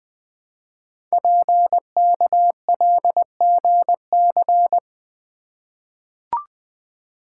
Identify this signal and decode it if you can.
Morse code, heard as a keyed tone, about 20 wpm, 700 Hz. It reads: PKLGC
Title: Code Groups - 5 Letter:Numbers - 20wpm